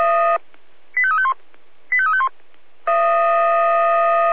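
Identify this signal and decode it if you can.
Five-tone selective call (ZVEI), heard as quick runs of five tones, tones of about 70 ms. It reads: 85351 85351
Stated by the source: ZVEI five-tone selective call 85211 received by SDR.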